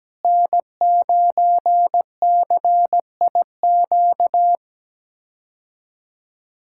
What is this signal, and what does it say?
Morse code, heard as a keyed tone, about 17 wpm, 700 Hz. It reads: N9CIQ
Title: K1USN SST QSO Practice - 17wpm